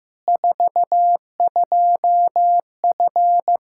Morse code, heard as a keyed tone, about 15 wpm, 700 Hz. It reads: 42F
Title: Practice QSO - Exchange 2 - Info - 15wpm